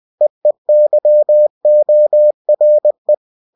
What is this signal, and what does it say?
Morse code, heard as a keyed tone, about 20 wpm, 600 Hz. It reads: EEYORE
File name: cw-books / winnie-the-pooh / winnie-20wpm-page-0050